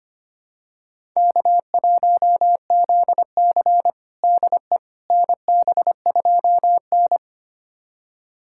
Morse code, heard as a keyed tone, about 25 wpm, 700 Hz. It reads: K1ZC DE NB2N
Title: USA Callsign DE Callsign - 4 Characters - 25wpm